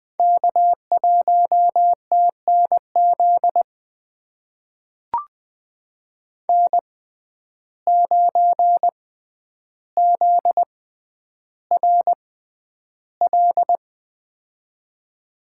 Morse code, heard as a keyed tone, about 20 wpm, spaced as for 10 wpm, 700 Hz. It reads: K1TNZ N9ZRL
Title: USA Call Signs - 5 Characters - 6x Character Spacing - 20wpm